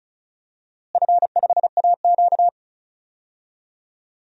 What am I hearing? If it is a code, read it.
Morse code, heard as a keyed tone, about 34 wpm, 700 Hz. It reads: F5AQ
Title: DXCC-100 Call Signs - 4 Characters - Repeated 3x - 35wpm